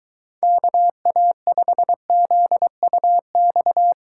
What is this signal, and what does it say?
Morse code, heard as a keyed tone, about 23 wpm, 700 Hz. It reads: KA5ZUX